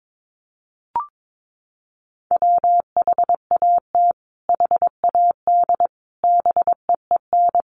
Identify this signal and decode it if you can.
Morse code, heard as a keyed tone, about 22 wpm, 700 Hz. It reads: WHAT HAD BEEN
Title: Sentences from Top 1500 Words 22wpm